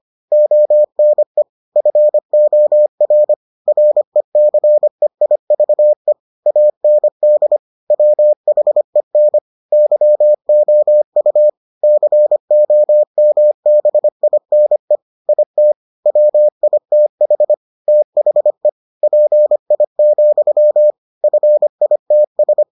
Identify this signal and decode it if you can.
Morse code, heard as a keyed tone, about 25 wpm, 600 Hz. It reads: ONE FOR RECEIVE AND WHEN YOU COMBINE IT WITH THE PI, FITS